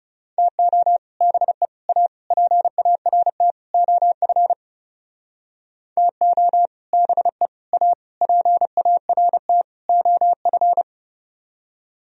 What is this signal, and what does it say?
Morse code, heard as a keyed tone, about 37 wpm, 700 Hz. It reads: TO BE A PART OF TO BE A PART OF
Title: Speed-Racing - Sets of 5 Words 30wpm